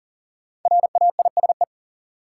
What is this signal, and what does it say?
Morse code, heard as a keyed tone, about 40 wpm, 700 Hz. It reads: RAISE
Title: Sentences from Top 1400 Words - New Word Drill - 40wpm